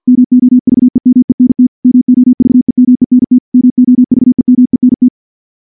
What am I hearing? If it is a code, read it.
Morse code, heard as a keyed tone, about 50 wpm, 250 Hz. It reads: MOVEMENT MOVEMENT MOVEMENT
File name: morse / morse-training-50WPM